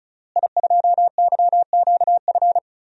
Morse code, heard as a keyed tone, about 36 wpm, 700 Hz. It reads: I2YQF
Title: DXCC-100 Call Signs - 5 Characters - Repeated 3x - 35wpm